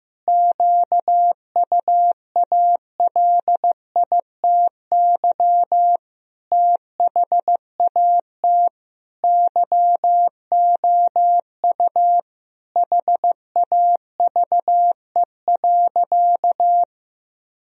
Morse code, heard as a keyed tone, about 15 wpm, 700 Hz. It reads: QUALITY THAT YOU HAVE.
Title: Sentences from Top 1000 Words - Review Part 4 - 15wpm